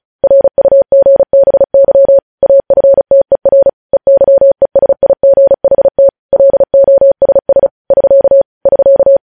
Morse code, heard as a keyed tone, about 36 wpm, 550 Hz. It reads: RUGBY AFTER EYESIGHT LOSS <SK> <SK>
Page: News Headlines in Morse Code at 35 WPM